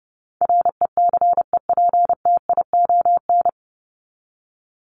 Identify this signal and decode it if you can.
Morse code, heard as a keyed tone, about 28 wpm, 700 Hz. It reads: RECEPTION
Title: Top 1751-2000 Words 30wpm